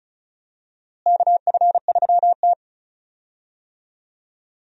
Morse code, heard as a keyed tone, about 36 wpm, 700 Hz. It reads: KF3T
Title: USA Call Signs - 4 Characters - Repeated 3x - 35wpm